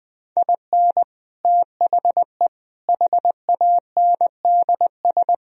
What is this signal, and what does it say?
Morse code, heard as a keyed tone, about 20 wpm, 700 Hz. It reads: IN THE HANDS